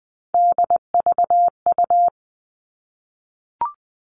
Morse code, heard as a keyed tone, about 20 wpm, 700 Hz. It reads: DVU
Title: All Cut Numbers - 3-Digit 20wpm